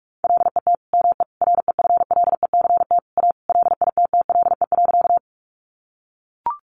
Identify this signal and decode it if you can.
Morse code, heard as a keyed tone, about 44 wpm, 700 Hz. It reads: LET ME REFLECT A LITTLE.
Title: Sentences from Top 1500 Words 45wpm